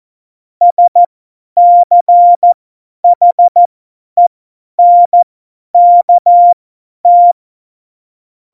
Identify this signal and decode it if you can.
Morse code, heard as a keyed tone, about 14 wpm, spaced as for 10 wpm, 700 Hz. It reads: SCHENKT
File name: Morsekurs / Witze / Witz 26. September 17, Tempo 50